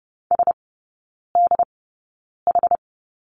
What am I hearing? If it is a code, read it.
Morse code, heard as a keyed tone, about 30 wpm, spaced as for 9 wpm, 700 Hz. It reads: SDH
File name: Lesson 19e - Letter w - Three Characters 30wpm